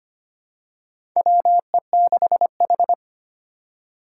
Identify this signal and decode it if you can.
Morse code, heard as a keyed tone, about 25 wpm, 700 Hz. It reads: WE6H
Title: USA Call Signs - 4 Characters - 4x Character Spacing - 25wpm